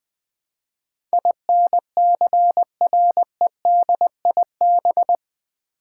Morse code, heard as a keyed tone, about 20 wpm, 700 Hz. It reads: INCREDIB
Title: Build A Word - 4 Letters At A Time - 10-Letter Words - 20wpm